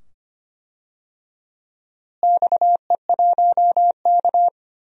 Morse code, heard as a keyed tone, about 25 wpm, 700 Hz. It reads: XE1K
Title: DXCC-100 Call Signs - 4 Characters - Repeated 2x - 25wpm